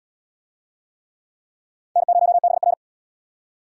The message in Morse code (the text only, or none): N0BR